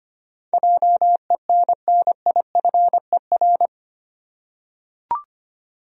Morse code, heard as a keyed tone, about 25 wpm, 700 Hz. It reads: JENNIFER
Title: USA Common Names - 25wpm